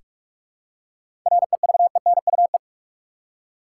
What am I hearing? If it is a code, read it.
Morse code, heard as a keyed tone, about 44 wpm, 700 Hz. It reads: REVENUE
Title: Seven Letter Words - 45wpm